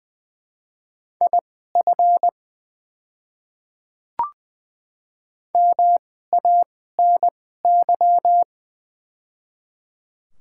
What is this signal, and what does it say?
Morse code, heard as a keyed tone, about 20 wpm, spaced as for 14 wpm, 690 Hz. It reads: IF MANY
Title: 2x Character Spacing - Top 500 Words - 20wpm